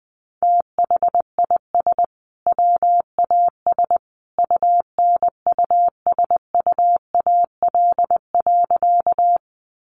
Morse code, heard as a keyed tone, about 20 wpm, 700 Hz. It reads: THIS WAS UNUSUAL.
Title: Sentences from Top 1400 Words 20wpm